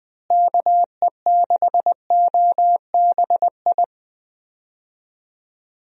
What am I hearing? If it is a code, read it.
Morse code, heard as a keyed tone, about 20 wpm, 700 Hz. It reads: KE6OBI
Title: USA Call Signs - 6 Characters - Repeated 3x - 20wpm